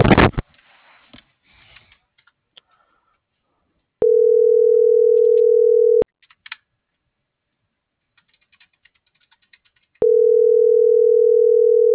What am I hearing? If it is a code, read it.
Ringback tone (North America), 440 + 480 Hz.